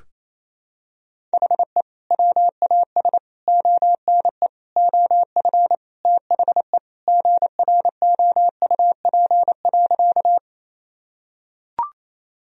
Morse code, heard as a keyed tone, about 28 wpm, 700 Hz. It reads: HE WAS ONE OF THE GROUP.
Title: Sentences from Top 1000 Words - Review Part 1 - 28wpm